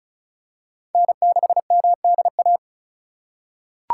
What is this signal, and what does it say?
Morse code, heard as a keyed tone, about 36 wpm, 700 Hz. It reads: N6MDA
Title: USA Call Signs - 5 Characters - 4x Character Spacing - 35wpm